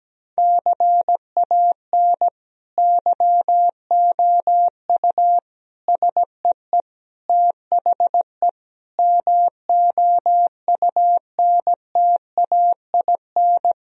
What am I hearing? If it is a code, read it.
Morse code, heard as a keyed tone, about 17 wpm, 700 Hz. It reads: CAN YOU SEE THE MOUNTAIN
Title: Sentences from Top 1000 Words - Review Part 1 - 17wpm